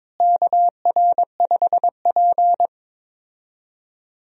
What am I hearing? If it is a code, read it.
Morse code, heard as a keyed tone, about 22 wpm, 700 Hz. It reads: KR5P
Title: USA Call Signs - 4 Characters - Repeated 3x - 22wpm